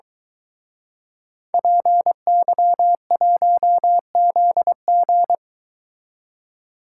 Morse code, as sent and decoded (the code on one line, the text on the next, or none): .--. -.-- .---- --.. --.
PY1ZG